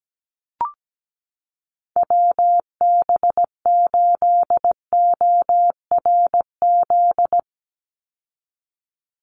Morse code, heard as a keyed tone, about 17 wpm, 700 Hz. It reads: WB8ORZ
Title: USA Call Signs 17wpm